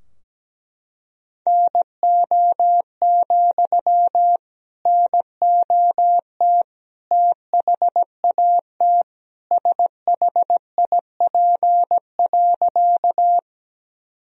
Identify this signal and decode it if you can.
Morse code, heard as a keyed tone, about 17 wpm, 700 Hz. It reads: NO, NOT THAT SHIP.
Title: Sentences from Top 1000 Words - Review Part 2 - 17wpm